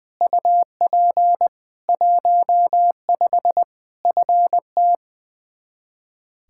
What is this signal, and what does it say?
Morse code, heard as a keyed tone, about 20 wpm, 700 Hz. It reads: UP 15 FT
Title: Practice QSO - Exchange 2 - Info - 20wpm